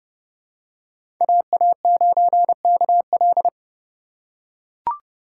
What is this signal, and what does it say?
Morse code, heard as a keyed tone, about 32 wpm, 700 Hz. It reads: AA9KL
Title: USA Call Signs - 5 Characters - 2x Character Spacing - 30wpm